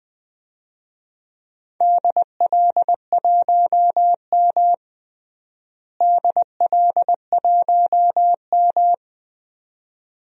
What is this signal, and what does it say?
Morse code, heard as a keyed tone, about 20 wpm, 700 Hz. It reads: DL1M DL1M
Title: DXCC-100 Call Signs - 4 Characters - Repeated 2x - 20wpm